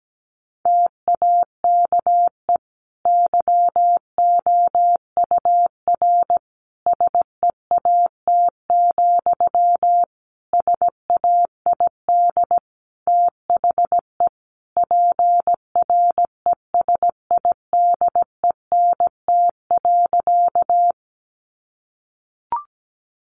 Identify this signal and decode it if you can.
Morse code, heard as a keyed tone, about 17 wpm, 700 Hz. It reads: TAKE YOUR SEAT, SAID THE PRESIDENT.
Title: Sentences from Top 1500 Words 17wpm